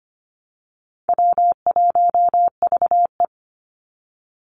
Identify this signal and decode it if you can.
Morse code, heard as a keyed tone, about 25 wpm, 700 Hz. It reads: W1VE
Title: Easy Call Signs Repeated 2x - 25wpm